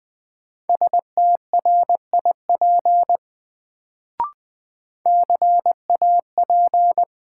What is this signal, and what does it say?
Morse code, heard as a keyed tone, about 20 wpm, 700 Hz. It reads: STRIP CAP
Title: Sentences from Top 1600 Words - New Word Drill - 20wpm